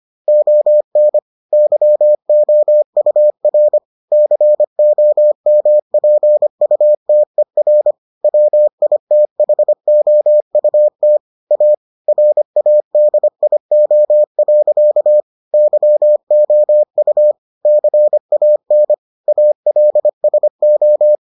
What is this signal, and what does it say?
Morse code, heard as a keyed tone, about 25 wpm, 600 Hz. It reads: ON YOUR COMPUTER WITHOUT A RADIO. YOU CAN ALSO